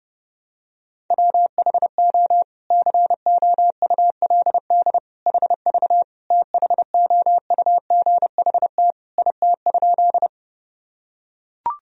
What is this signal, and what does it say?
Morse code, heard as a keyed tone, about 32 wpm, 700 Hz. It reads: WHO COULD HV THOUGHT IT?
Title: Sentences from Top 300 Words and Abbreviations 30wpm